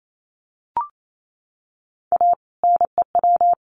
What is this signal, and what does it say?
Morse code, heard as a keyed tone, about 28 wpm, 700 Hz. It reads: A NEW
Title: Sets of 2 Words 28wpm